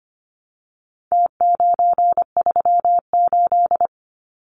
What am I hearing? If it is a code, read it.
Morse code, heard as a keyed tone, about 25 wpm, 700 Hz. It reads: T938
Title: Numbers - Contest Style - 4-Digit 25wpm